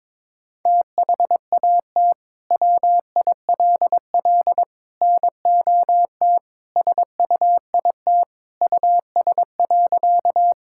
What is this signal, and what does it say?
Morse code, heard as a keyed tone, about 22 wpm, 700 Hz. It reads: THAT WILL NOT SUIT US.
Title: Sentences from Top 1600 Words 22wpm